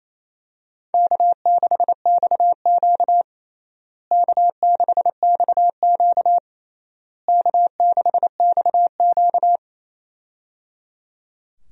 Morse code, heard as a keyed tone, about 27 wpm, 700 Hz. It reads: K6XQ K6XQ K6XQ
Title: USA Call Signs - 4 Characters - Repeated 3x - 28wpm